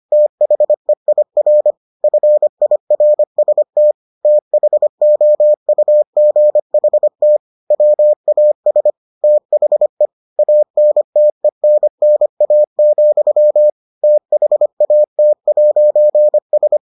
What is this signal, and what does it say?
Morse code, heard as a keyed tone, about 25 wpm, 600 Hz. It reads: THEIR FIRST THOUGHT WAS THE ANTENNA, THAT'S